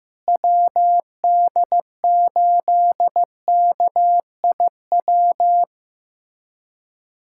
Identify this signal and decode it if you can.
Morse code, heard as a keyed tone, about 15 wpm, 700 Hz. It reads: WD8KIW
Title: USA Callsign DE Callsign - 6 Characters - 15wpm